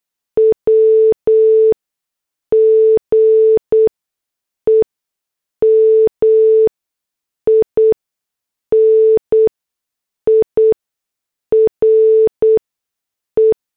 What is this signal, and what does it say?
Morse code, heard as a keyed tone, about 8 wpm, 440 Hz. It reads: W G E M I N I R E
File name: morse / headlines